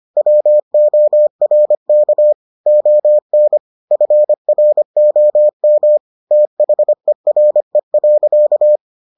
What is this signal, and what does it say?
Morse code, heard as a keyed tone, about 25 wpm, 600 Hz. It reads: WORK ON FROM THERE.